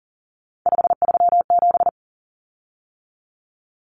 Morse code, heard as a keyed tone, about 40 wpm, 700 Hz. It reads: 537
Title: Common Cut Numbers - 3-Digit 40wpm